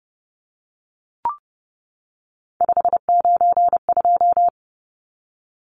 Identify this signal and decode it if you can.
Morse code, heard as a keyed tone, about 30 wpm, 700 Hz. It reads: 592
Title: Numbers - 3-Digit 30wpm